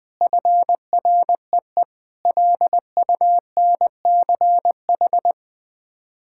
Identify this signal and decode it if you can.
Morse code, heard as a keyed tone, about 20 wpm, 700 Hz. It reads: FREE LUNCH
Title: Sets of 9 Words - 20wpm